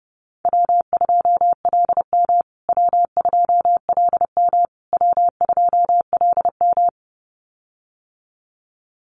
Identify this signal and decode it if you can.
Morse code, heard as a keyed tone, about 30 wpm, 700 Hz. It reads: W2LM W2LM W2LM
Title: USA Call Signs Repeated 3x 30wpm